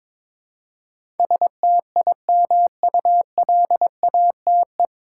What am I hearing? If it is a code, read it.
Morse code, heard as a keyed tone, about 22 wpm, 700 Hz. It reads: STIMULATE
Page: Morse Code Ninja practice 22wpm